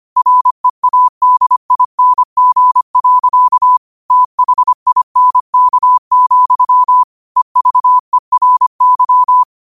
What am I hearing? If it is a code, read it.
Morse code, heard as a keyed tone, about 25 wpm, 1000 Hz. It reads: READING. THINK, EVERY